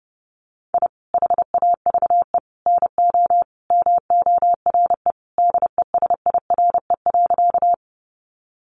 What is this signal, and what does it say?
Morse code, heard as a keyed tone, about 30 wpm, 700 Hz. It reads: I HAVE NO MORE DESIRE.
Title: Sentences from Top 1100 Words 30wpm